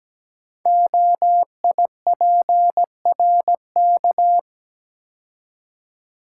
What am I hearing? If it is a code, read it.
Morse code, heard as a keyed tone, about 17 wpm, 700 Hz. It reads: OIPRK
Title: Code Groups - 5 Letters - 17wpm